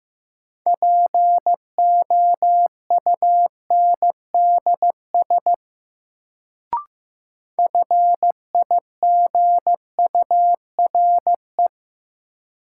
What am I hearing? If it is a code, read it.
Morse code, heard as a keyed tone, about 15 wpm, 700 Hz. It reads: POUNDS FIGURE
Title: Sentences from Top 1000 Words - Review Part 2 - New Word Drill - 15wpm